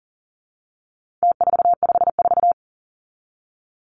Morse code, heard as a keyed tone, about 40 wpm, 700 Hz. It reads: T454
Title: Numbers - Contest Style - 4-Digit 40wpm